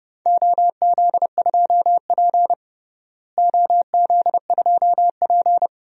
Morse code, heard as a keyed tone, about 32 wpm, 700 Hz. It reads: OZ2P OZ2P
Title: DXCC-100 Call Signs - 4 Characters - Repeated 3x - 30wpm